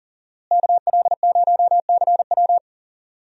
Morse code, heard as a keyed tone, about 41 wpm, 700 Hz. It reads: KL0CW
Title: USA Call Signs - 5 Characters - Repeated 3x - 40wpm